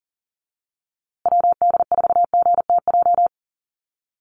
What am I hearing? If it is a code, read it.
Morse code, heard as a keyed tone, about 40 wpm, 700 Hz. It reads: WD4GTJ
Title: USA Call Signs Repeated 2x 40wpm